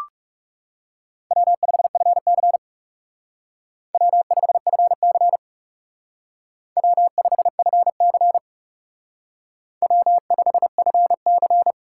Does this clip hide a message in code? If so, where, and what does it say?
Yes: Morse code from 1.3 to 11.7 s, heard as a keyed tone, about 39 wpm, 700 Hz. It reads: W5FC W5FC W5FC W5FC